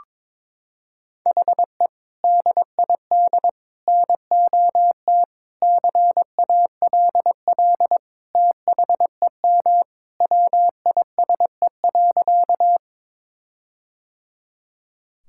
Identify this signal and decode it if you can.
Morse code, heard as a keyed tone, about 21 wpm, 700 Hz. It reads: HE DID NOT CALL THEM WISE.
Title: Sentences from Top 1600 Words 22wpm